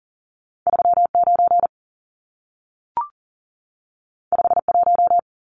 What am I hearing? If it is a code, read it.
Morse code, heard as a keyed tone, about 40 wpm, 700 Hz. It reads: 39 51
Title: Numbers - 2-Digit 40wpm